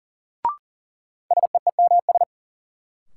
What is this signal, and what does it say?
Morse code, heard as a keyed tone, about 40 wpm, 700 Hz. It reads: SEEMS
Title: Sets of 3 Words - New Word Drill - 40wpm